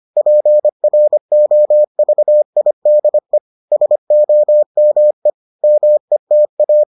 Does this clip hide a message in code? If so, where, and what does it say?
Yes: Morse code from 0.2 to 6.8 s, heard as a keyed tone, about 25 wpm, 600 Hz. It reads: PROVIDE SOME META